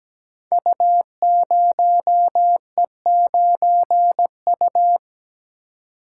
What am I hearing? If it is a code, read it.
Morse code, heard as a keyed tone, about 17 wpm, 700 Hz. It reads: U0E9U
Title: Code Groups - 5 Letter:Numbers - 17wpm